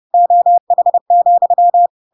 Morse code, heard as a keyed tone, about 30 wpm, 700 Hz. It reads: OH,
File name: Pinocchio / Pinocchio ch1 30 30 ik1hge